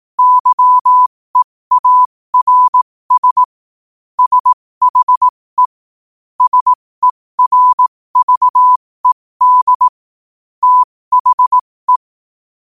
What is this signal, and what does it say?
Morse code, heard as a keyed tone, about 18 wpm, spaced as for 15 wpm, 1000 Hz. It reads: YEARS SHE SERVED THE
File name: qotdfiles / 15wpm